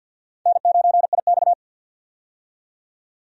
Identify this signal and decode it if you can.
Morse code, heard as a keyed tone, about 53 wpm, 700 Hz. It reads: N9IX
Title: USA Call Signs - 4 Characters - 50wpm